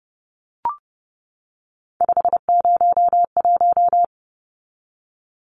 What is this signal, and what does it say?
Morse code, heard as a keyed tone, about 30 wpm, 700 Hz. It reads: 501
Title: Numbers - 3-Digit 30wpm